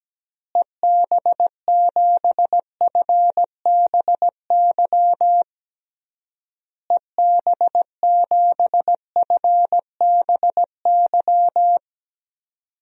Morse code, heard as a keyed tone, about 17 wpm, 700 Hz. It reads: EB7FBY EB7FBY
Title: DXCC-100 Call Signs - 6 Characters - Repeated 3x - 17wpm